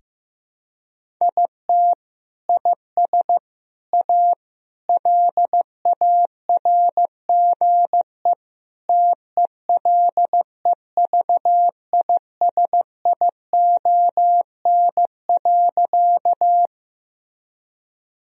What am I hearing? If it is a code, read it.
Morse code, heard as a keyed tone, about 15 wpm, 700 Hz. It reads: IT IS A LARGE TELEVISION.
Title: Sentences from Top 1000 Words - Review Part 4 - 15wpm